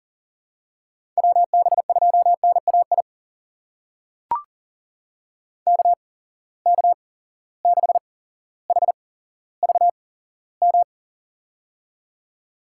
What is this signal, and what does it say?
Morse code, heard as a keyed tone, about 40 wpm, spaced as for 17 wpm, 700 Hz. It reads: WB2NAI KK6HVM
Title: USA Call Signs - 6 Characters - 8x Character Spacing - 40wpm